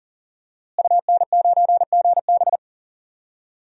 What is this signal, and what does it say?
Morse code, heard as a keyed tone, about 40 wpm, 700 Hz. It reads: UN9GB
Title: DXCC-100 Call Signs - 5 Characters - Repeated 2x - 40wpm